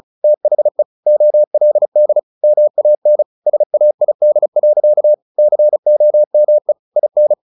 Morse code, heard as a keyed tone, about 34 wpm, 600 Hz. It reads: THE OLD MAN SAID. COME IN
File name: the old man and the sea 2 35wpm